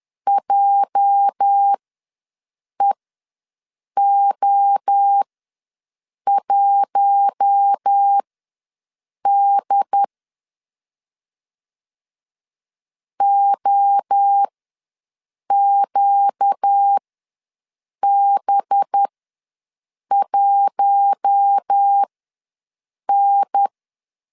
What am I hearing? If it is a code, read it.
Morse code, heard as a keyed tone, about 10 wpm, spaced as for 6 wpm, 800 Hz. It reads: JEO1D OQB1N
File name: LEZIONI / 8 YJ45 / 8 YJ ws 10wpm 4043